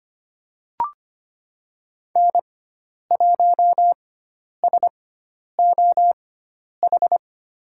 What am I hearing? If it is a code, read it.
Morse code, heard as a keyed tone, about 25 wpm, spaced as for 10 wpm, 700 Hz. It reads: N1SOH
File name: Lesson 16g - Letter c - Call Signs 25wpm